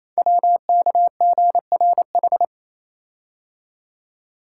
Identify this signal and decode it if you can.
Morse code, heard as a keyed tone, about 27 wpm, 700 Hz. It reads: WKGRH
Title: Code Groups - 5 Letters - 28wpm